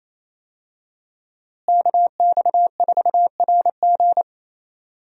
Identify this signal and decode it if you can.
Morse code, heard as a keyed tone, about 28 wpm, 700 Hz. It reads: KX4RG
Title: USA Call Signs - 5 Characters - Repeated 2x - 28wpm